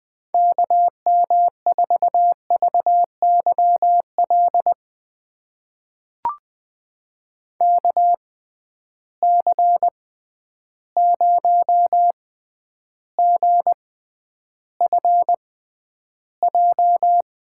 Morse code, heard as a keyed tone, about 20 wpm, spaced as for 10 wpm, 700 Hz. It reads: KM4VYL KC0GFJ
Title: USA Call Signs - 6 Characters - 6x Character Spacing - 20wpm